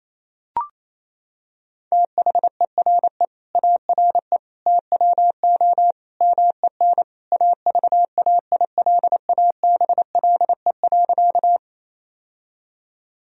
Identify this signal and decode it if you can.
Morse code, heard as a keyed tone, about 28 wpm, 700 Hz. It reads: THERE ARE TWO MEN AVAILABLE.
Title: Sentences from Top 1000 Words - Review Part 3 - 28wpm